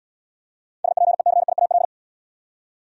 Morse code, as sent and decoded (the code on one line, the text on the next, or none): ... .--. -.-. .. .- .-..
SPECIAL